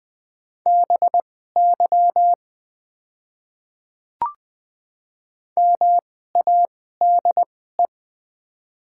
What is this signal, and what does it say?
Morse code, heard as a keyed tone, about 20 wpm, spaced as for 14 wpm, 700 Hz. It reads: BY MADE
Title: 2x Character Spacing - Top 200 Words - 20wpm